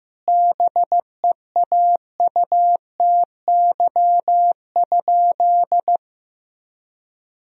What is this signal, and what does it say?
Morse code, heard as a keyed tone, about 15 wpm, 700 Hz. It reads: BEAUTY?
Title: Sentences from Top 1000 Words - Review Part 2 - 15wpm